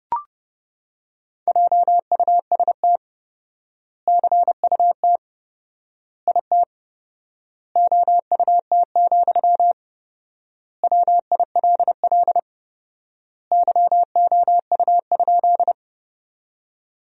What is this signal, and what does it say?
Morse code, heard as a keyed tone, about 30 wpm, 700 Hz. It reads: JUST CUT IT OUT, WILL YOU?
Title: 4x Word-Spacing - Sentences from Top 300 Words - 30wpm